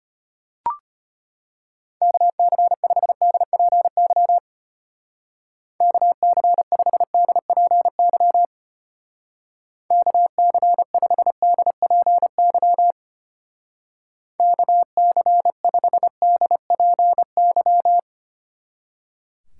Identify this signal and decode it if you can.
Morse code, heard as a keyed tone, about 31 wpm, 700 Hz. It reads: KC5DPY KC5DPY KC5DPY KC5DPY
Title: Speed Racing - USA 2x3 Call Signs 25wpm